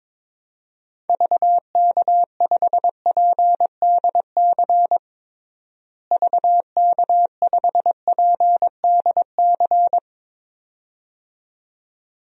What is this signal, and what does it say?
Morse code, heard as a keyed tone, about 21 wpm, 700 Hz. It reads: VK5PDC VK5PDC
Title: DXCC-100 Call Signs - 6 Characters - Repeated 2x - 22wpm